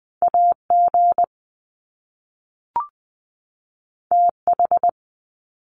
Morse code, heard as a keyed tone, about 20 wpm, 700 Hz. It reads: AG TH
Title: Top Two Letter Combinations in Words 20wpm